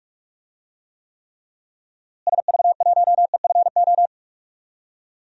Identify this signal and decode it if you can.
Morse code, heard as a keyed tone, about 43 wpm, 700 Hz. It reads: SV1EFO
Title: DXCC-100 Call Signs - 6 Characters - Repeated 3x - 45wpm